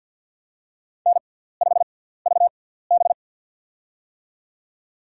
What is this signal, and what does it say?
Morse code, heard as a keyed tone, about 51 wpm, spaced as for 17 wpm, 690 Hz. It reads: N5VB
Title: USA Call Signs - 4 Characters - 6x Character Spacing - 50wpm